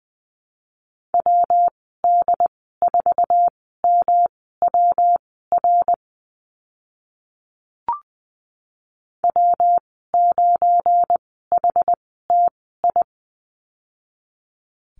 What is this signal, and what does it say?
Morse code, heard as a keyed tone, about 20 wpm, spaced as for 14 wpm, 700 Hz. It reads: WD4MWR W9HTI
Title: 193 Lesson 31g - Letter v - Call Signs 20wpm